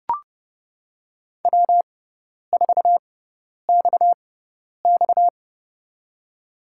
Morse code, heard as a keyed tone, about 30 wpm, spaced as for 10 wpm, 700 Hz. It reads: W4XX
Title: USA Call Signs - 6x Character Spacing - 30wpm